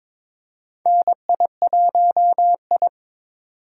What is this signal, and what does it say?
Morse code, heard as a keyed tone, about 22 wpm, 700 Hz. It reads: NI1I
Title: USA Call Signs - 4 Characters - 2x Character Spacing - 22wpm